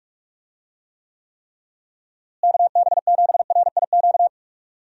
Morse code, heard as a keyed tone, about 45 wpm, 700 Hz. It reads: KB7RIQ
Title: USA Call Signs - 6 Characters - Repeated 3x - 45wpm